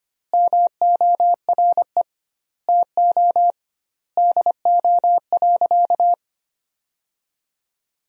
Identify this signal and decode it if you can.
Morse code, heard as a keyed tone, about 25 wpm, 700 Hz. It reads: MORE TO DO.
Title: 2x Word-Spacing - Sentences from Top 100 Words - 25wpm